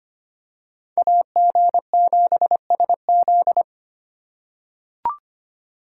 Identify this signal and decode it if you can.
Morse code, heard as a keyed tone, about 25 wpm, 700 Hz. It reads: AG7SZ
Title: USA Call Signs - 5 Characters - 2x Character Spacing - 25wpm